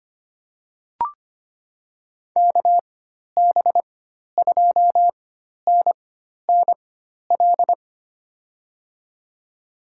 Morse code, heard as a keyed tone, about 25 wpm, spaced as for 12 wpm, 700 Hz. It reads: KB2NNL
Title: USA Call Signs - 6 Characters - 4x Character Spacing - 25wpm